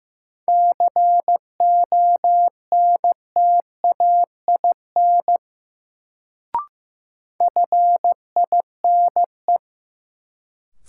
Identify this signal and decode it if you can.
Morse code, heard as a keyed tone, about 16 wpm, 700 Hz. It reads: CONTAIN FINE
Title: Sentences from Top 500 Words - New Word Drill - 15wpm